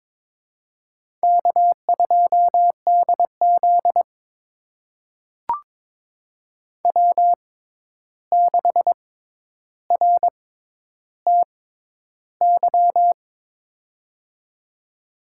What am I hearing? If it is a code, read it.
Morse code, heard as a keyed tone, about 22 wpm, spaced as for 11 wpm, 700 Hz. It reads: K2DZ W6RTY
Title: USA Call Signs - 6x Character Spacing - 22wpm